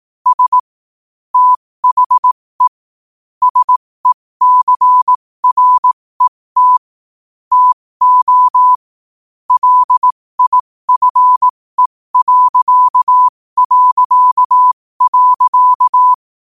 Morse code, heard as a keyed tone, about 18 wpm, spaced as for 15 wpm, 1000 Hz. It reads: S THE SECRET TO LIFE...
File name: qotdfiles / 15wpm